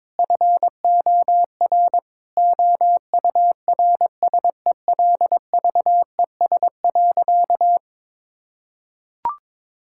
Morse code, heard as a keyed tone, about 21 wpm, 700 Hz. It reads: FOR OURSELVES.
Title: Sentences from Top 1600 Words 22wpm